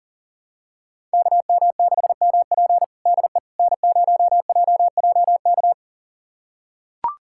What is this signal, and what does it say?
Morse code, heard as a keyed tone, about 40 wpm, 700 Hz. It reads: KM6MP DE N0JJK
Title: USA Callsign DE Callsign - 5 Characters - 40wpm